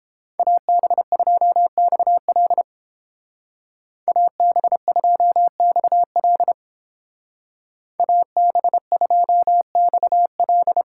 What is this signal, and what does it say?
Morse code, heard as a keyed tone, about 29 wpm, 700 Hz. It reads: AB2XL AB2XL AB2XL